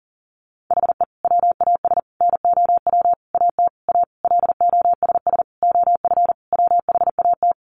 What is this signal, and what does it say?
Morse code, heard as a keyed tone, about 40 wpm, 700 Hz. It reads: HE WAS NOW AT A LOSS OF WHAT
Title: Sentences from Top 1400 Words 40wpm